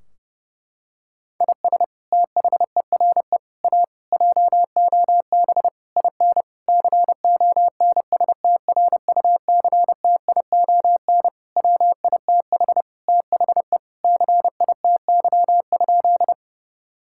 Morse code, heard as a keyed tone, about 30 wpm, 700 Hz. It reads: IS THERE A JOB IN CONSTRUCTION WITH THE CITY?
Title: Sentences from Top 1600 Words 30wpm